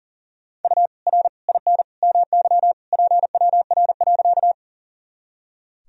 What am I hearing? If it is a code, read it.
Morse code, heard as a keyed tone, about 40 wpm, 700 Hz. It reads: U R IN MY PWR.
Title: Sentences from Top 500 Words and Abbreviations 40wpm